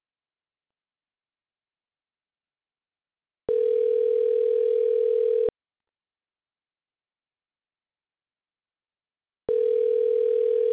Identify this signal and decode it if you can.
Ringback tone (North America), 440 + 480 Hz.